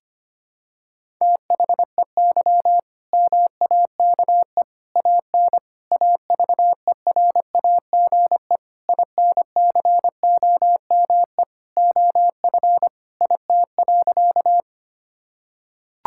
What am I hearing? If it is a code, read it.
Morse code, heard as a keyed tone, about 25 wpm, 700 Hz. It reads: THEY MAKE AN AVERAGE INCOME OF IT.
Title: Sentences from Top 1000 Words - Review Part 6 - 25wpm